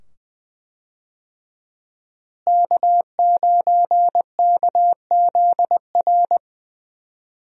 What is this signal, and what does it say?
Morse code, heard as a keyed tone, about 20 wpm, 700 Hz. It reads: K9KZR
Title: USA Call Signs - 5 Characters - Repeated 2x - 20wpm